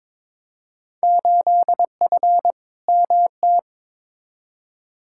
Morse code, heard as a keyed tone, about 21 wpm, 700 Hz. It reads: 8F MT
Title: Field Day Exchanges Encore - 22wpm